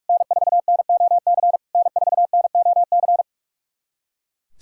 Morse code, heard as a keyed tone, about 42 wpm, 690 Hz. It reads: N4NOC N4NOC
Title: SOTA QSO Practice 45wpm